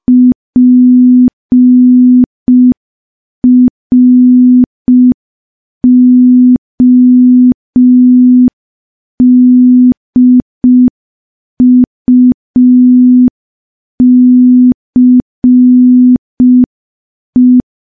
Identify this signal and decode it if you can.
Morse code, heard as a keyed tone, about 5 wpm, 250 Hz. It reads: PRODUCE
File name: morse / morse-training-05WPM